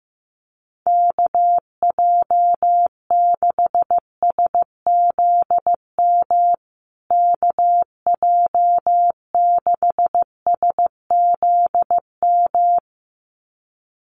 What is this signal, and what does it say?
Morse code, heard as a keyed tone, about 15 wpm, 700 Hz. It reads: KJ6SZM KJ6SZM
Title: USA Call Signs Repeated 2x 15wpm